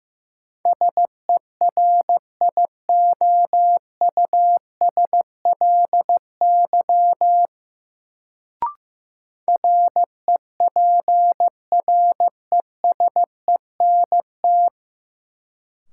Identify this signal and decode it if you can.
Morse code, heard as a keyed tone, about 15 wpm, 700 Hz. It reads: SERIOUSLY REPRESENT
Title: Sentences from Top 1200 Words - New Word Drill Alt - 15wpm